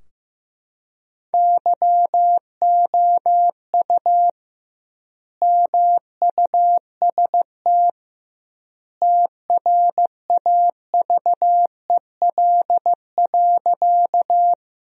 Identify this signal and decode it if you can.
Morse code, heard as a keyed tone, about 15 wpm, 700 Hz. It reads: YOU MUST TRAVEL.
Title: 2x Word-Spacing - Sentences from Top 400 Words - 15wpm